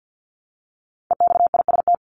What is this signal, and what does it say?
Morse code, heard as a keyed tone, about 50 wpm, 700 Hz. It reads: EXIST
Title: ING Suffix - Top 100 Words with Base Words 50wpm